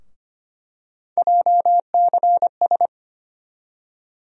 Morse code, heard as a keyed tone, about 25 wpm, 700 Hz. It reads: JCS
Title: USA Call Sign Postfixes - 3 Letters 25wpm